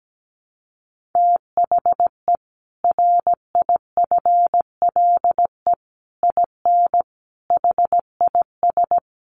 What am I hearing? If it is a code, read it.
Morse code, heard as a keyed tone, about 17 wpm, 700 Hz. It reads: THE RIFLE IN HIS
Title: A Sound Of Thunder 17wpm